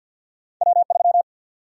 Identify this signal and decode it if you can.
Morse code, heard as a keyed tone, about 53 wpm, 700 Hz. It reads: W3
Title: USA Call Sign Prefixes - 50wpm